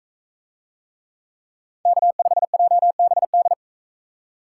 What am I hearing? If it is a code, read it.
Morse code, heard as a keyed tone, about 42 wpm, 700 Hz. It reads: K5JBD